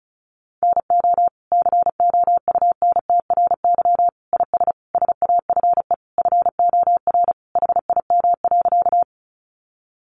Morse code, heard as a keyed tone, about 34 wpm, 700 Hz. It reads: NO COUNTRY IS SAFE FOR HIM.
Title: Sentences from Top 900 Words 35wpm